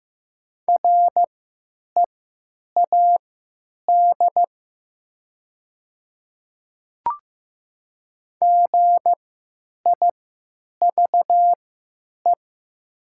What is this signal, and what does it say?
Morse code, heard as a keyed tone, about 15 wpm, spaced as for 9 wpm, 700 Hz. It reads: READ GIVE
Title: 3x Character Spacing - Top 300 Words - 15wpm